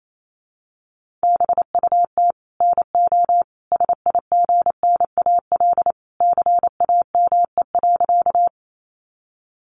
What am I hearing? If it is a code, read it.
Morse code, heard as a keyed tone, about 28 wpm, 700 Hz. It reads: BUT NO SIGNAL CAME.
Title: Sentences from Top 1400 Words 28wpm